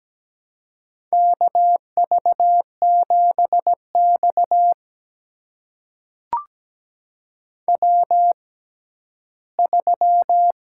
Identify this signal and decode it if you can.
Morse code, heard as a keyed tone, about 17 wpm, spaced as for 12 wpm, 700 Hz. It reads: KV7X W3
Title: USA Call Signs - 4 Characters - 6x Character Spacing - 17wpm